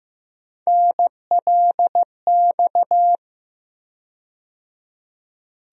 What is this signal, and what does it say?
Morse code, heard as a keyed tone, about 15 wpm, 700 Hz. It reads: NLX
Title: USA Call Sign Postfixes - 3 Letters 15wpm